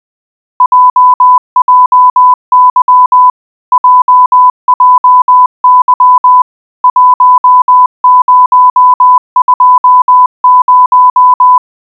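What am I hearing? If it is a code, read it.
Morse code, heard as a keyed tone, about 20 wpm, 1000 Hz. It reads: JJY JJY 1020
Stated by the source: Software-generated Morse recreating the JJY shortwave standard time station's call sign, time and radio warning signals, generated with cwwav.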